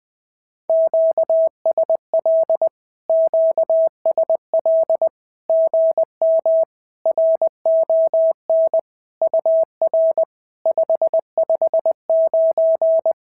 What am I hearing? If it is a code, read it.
Morse code, heard as a keyed tone, about 19 wpm, 650 Hz. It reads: QSL QSL GM RON UR 559
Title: SOTA QSO Practice 20wpm